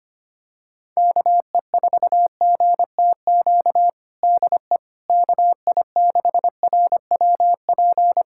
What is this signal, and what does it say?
Morse code, heard as a keyed tone, about 25 wpm, 700 Hz. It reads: KE4GTQ DE KI6RWP
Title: USA Callsign DE Callsign - 6 Characters - 25wpm